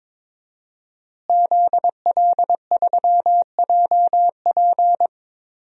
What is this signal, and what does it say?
Morse code, heard as a keyed tone, about 22 wpm, 700 Hz. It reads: ZL3JP
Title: DXCC-100 Call Signs - 5 Characters - Repeated 2x - 22wpm